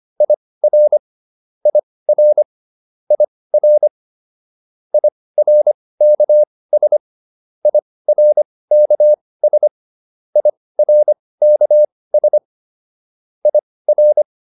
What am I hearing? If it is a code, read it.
Morse code, heard as a keyed tone, about 25 wpm, spaced as for 18 wpm, 600 Hz. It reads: IR IR IR IRKS IRKS IRKS IR